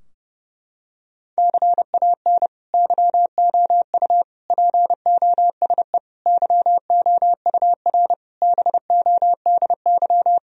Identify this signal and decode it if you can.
Morse code, heard as a keyed tone, about 30 wpm, 700 Hz. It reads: CAN YOU POSE YOUR BODY
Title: Sentences from Top 1000 Words - Review Part 2 - 30wpm